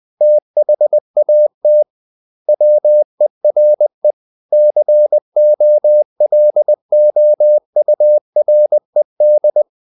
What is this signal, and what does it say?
Morse code, heard as a keyed tone, about 20 wpm, 600 Hz. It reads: THAT WERE COLOURED
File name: the old man and the sea / the old man and the sea part 1 20wpm